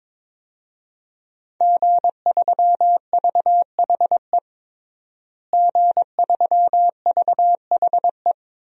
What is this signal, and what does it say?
Morse code, heard as a keyed tone, about 22 wpm, 700 Hz. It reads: G3VHE G3VHE
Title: DXCC-100 Call Signs - 5 Characters - Repeated 2x - 22wpm